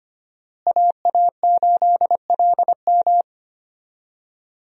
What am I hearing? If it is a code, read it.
Morse code, heard as a keyed tone, about 25 wpm, 700 Hz. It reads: AA8LM
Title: USA Call Signs - 5 Characters - Repeated 2x - 25wpm